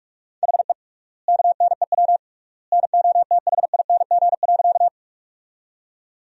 Morse code, heard as a keyed tone, about 46 wpm, 700 Hz. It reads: HE KNEW NOTHING.
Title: 3x Word-Spacing - Sentences from Top 500 Words - 45wpm